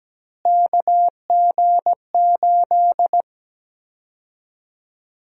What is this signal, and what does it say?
Morse code, heard as a keyed tone, about 17 wpm, 700 Hz. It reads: KG8
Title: USA Call Sign Prefixes - Repeated 2x - 17wpm